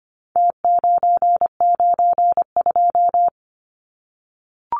Morse code, heard as a keyed tone, about 25 wpm, 700 Hz. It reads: T992
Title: Numbers - Contest Style - 4-Digit 25wpm